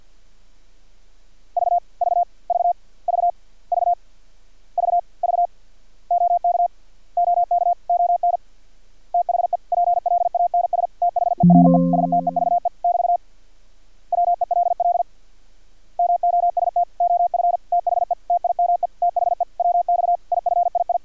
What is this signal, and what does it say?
Morse code, heard as a keyed tone, about 50 wpm, 700 Hz. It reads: V V V V V V V OK OK ON THE PLANS THERE STEVE <BT> WELL MOST OF THE TIME THE WX IFII
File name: archived / CRSnet